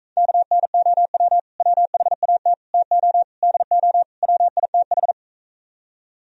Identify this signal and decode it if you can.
Morse code, heard as a keyed tone, about 44 wpm, 700 Hz. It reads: KNOW WHAT TO DO WITH